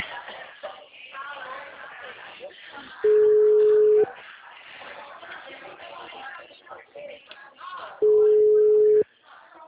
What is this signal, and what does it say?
Ringback tone (Europe, 425 Hz), with speech.
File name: out- -2010- - - .19001